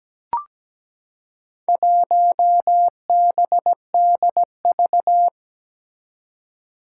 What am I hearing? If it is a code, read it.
Morse code, heard as a keyed tone, about 17 wpm, 700 Hz. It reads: 1BDV
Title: Four Challenging Letter-Number Combinations 17wpm